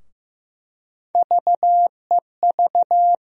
Morse code, heard as a keyed tone, about 16 wpm, 700 Hz. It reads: VEV
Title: USA Call Sign Postfixes - 3 Letters 15wpm